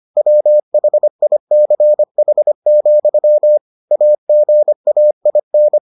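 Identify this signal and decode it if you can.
Morse code, heard as a keyed tone, about 25 wpm, 600 Hz. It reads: WHICH, AGAIN
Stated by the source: Morse edition of the Foundations of Amateur Radio episodes.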